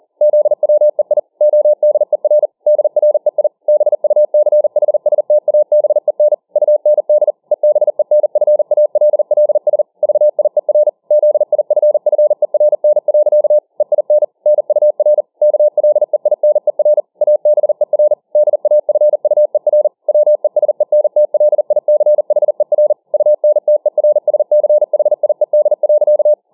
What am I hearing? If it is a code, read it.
Morse code, heard as a keyed tone, about 40 wpm, 600 Hz. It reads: ZWEI ODER DREI BUCHSTABEN UND EBENFALLS VIER ZIFFERN. EIN NUR KLEINER ABER DAFUER WESENTLICHER UNTERSCHIED.